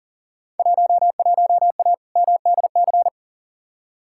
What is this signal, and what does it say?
Morse code, heard as a keyed tone, about 37 wpm, 700 Hz. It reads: 11A MDC
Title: Field Day Exchanges Encore - 40wpm